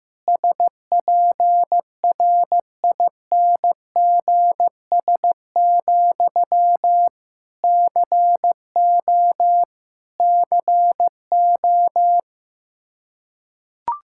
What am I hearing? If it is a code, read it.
Morse code, heard as a keyed tone, about 15 wpm, 700 Hz. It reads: SPRINGS, CO CO
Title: K1USN SST QSO Practice 15wpm